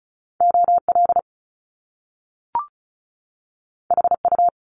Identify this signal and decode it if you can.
Morse code, heard as a keyed tone, about 36 wpm, 700 Hz. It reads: OL HU
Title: Top Two Letter Combinations in Words 35wpm